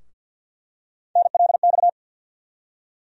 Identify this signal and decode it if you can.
Morse code, heard as a keyed tone, about 50 wpm, 700 Hz. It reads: NLX
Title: USA Call Sign Postfixes - 3 Letters 50wpm